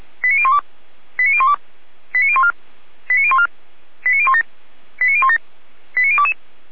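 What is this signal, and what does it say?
Five-tone selective call (ZVEI), heard as quick runs of five tones, tones of about 70 ms. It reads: 89012 89012 89014 89014 89017 89017 89022
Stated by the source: ZVEI five-tone call received by SDR.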